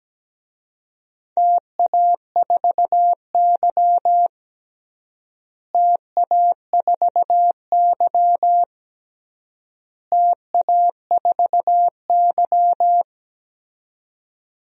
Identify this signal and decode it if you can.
Morse code, heard as a keyed tone, about 17 wpm, 700 Hz. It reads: TA4Y TA4Y TA4Y
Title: DXCC-100 Call Signs - 4 Characters - Repeated 3x - 17wpm